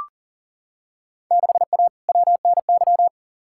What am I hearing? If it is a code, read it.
Morse code, heard as a keyed tone, about 40 wpm, 700 Hz. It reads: 6A WNY
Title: Field Day Exchanges - 40wpm